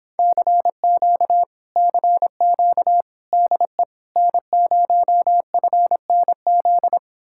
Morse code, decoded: CQ CQ DE N0FNZ

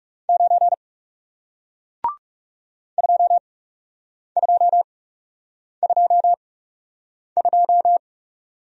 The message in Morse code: 9 2222

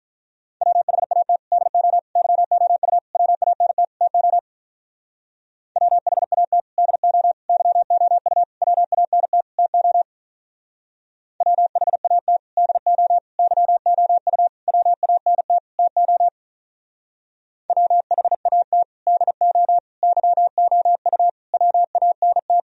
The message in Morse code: WHAT DO YOU WANT TO WHAT DO YOU WANT TO WHAT DO YOU WANT TO WHAT DO YOU WANT

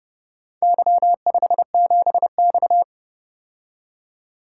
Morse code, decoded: Y57X

